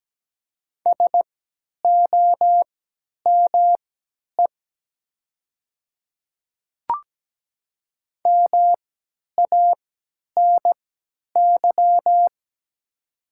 Morse code, heard as a keyed tone, about 17 wpm, spaced as for 10 wpm, 700 Hz. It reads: SOME MANY
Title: 3x Character Spacing - Top 200 Words - 17wpm